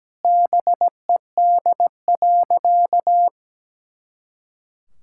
Morse code, heard as a keyed tone, about 17 wpm, 700 Hz. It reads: BED.
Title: Pangram Challenge v2 - 17wpm